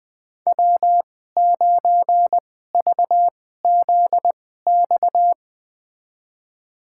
Morse code, heard as a keyed tone, about 20 wpm, spaced as for 14 wpm, 700 Hz. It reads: W9VZX
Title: USA Call Signs - 5 Characters - 2x Character Spacing - 20wpm